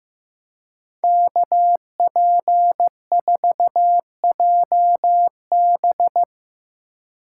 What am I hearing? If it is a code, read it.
Morse code, heard as a keyed tone, about 16 wpm, 700 Hz. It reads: KP4JB
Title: Speed Racing - USA 2x2 Call Signs 15wpm